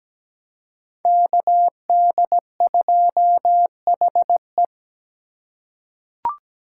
Morse code, heard as a keyed tone, about 17 wpm, 700 Hz. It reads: KD2HE
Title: K1USN SST QSO Practice - 17wpm